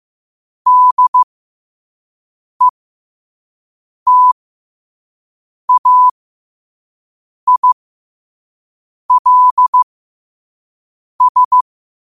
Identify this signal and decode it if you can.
Morse code, heard as a keyed tone, about 15 wpm, spaced as for 5 wpm, 1000 Hz. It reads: DETAILS
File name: qotdfiles / 5wpm